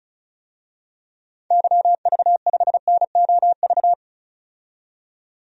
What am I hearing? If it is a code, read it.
Morse code, heard as a keyed tone, about 35 wpm, 700 Hz. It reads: YV5NOV
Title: DXCC-100 Call Signs - 6 Characters - 35wpm